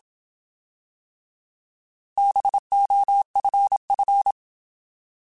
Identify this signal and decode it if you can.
Morse code, heard as a keyed tone, about 27 wpm, 800 Hz. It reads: BOFF
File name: Alphabetical / Alphabetical mp3s / Bs @25